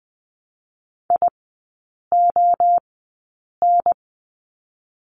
Morse code, heard as a keyed tone, about 20 wpm, 700 Hz. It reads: I O N
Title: 029 Lesson 6e - Letter i - Three Characters 20wpm